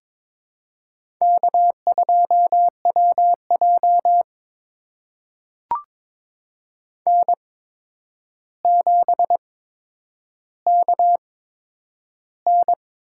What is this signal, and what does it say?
Morse code, heard as a keyed tone, about 22 wpm, spaced as for 9 wpm, 700 Hz. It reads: K2WJ N7KN
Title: USA Call Signs - 4 Characters - 8x Character Spacing - 22wpm